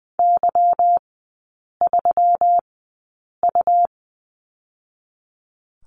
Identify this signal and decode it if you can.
Morse code, heard as a keyed tone, about 20 wpm, spaced as for 8 wpm, 700 Hz. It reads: Y3U
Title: 145 Lesson 24e - Letter y - Three Characters 20wpm